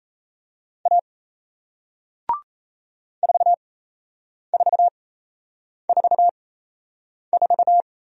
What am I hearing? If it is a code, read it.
Morse code, heard as a keyed tone, about 38 wpm, spaced as for 8 wpm, 700 Hz. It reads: A 4444